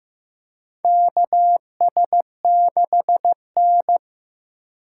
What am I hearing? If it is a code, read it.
Morse code, heard as a keyed tone, about 15 wpm, 700 Hz. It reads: KS6N
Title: USA Call Signs - 4 Characters - 2x Character Spacing - 15wpm